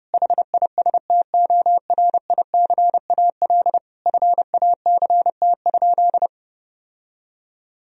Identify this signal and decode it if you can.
Morse code, heard as a keyed tone, about 30 wpm, 700 Hz. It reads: HISTORICAL FACT?
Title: Sentences from Top 1000 Words - Review Part 4 - 30wpm